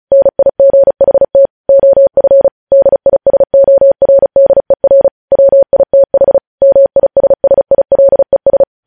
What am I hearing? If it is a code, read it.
Morse code, heard as a keyed tone, about 36 wpm, 550 Hz. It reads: NIGHT OF DISORDER WITH MISSILES
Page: news headlines in morse code at 35 wpm